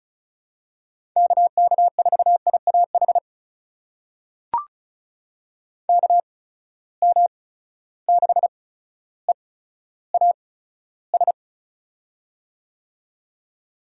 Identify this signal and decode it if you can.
Morse code, heard as a keyed tone, about 36 wpm, spaced as for 15 wpm, 700 Hz. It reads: KK4IAH KM6EAS
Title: USA Call Signs - 6 Characters - 8x Character Spacing - 35wpm